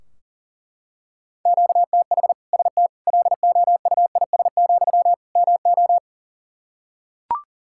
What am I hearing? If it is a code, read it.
Morse code, heard as a keyed tone, about 40 wpm, 700 Hz. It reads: QTH ST LOUIS, MO
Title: Practice QSO - Exchange 1 - Beginning - 40wpm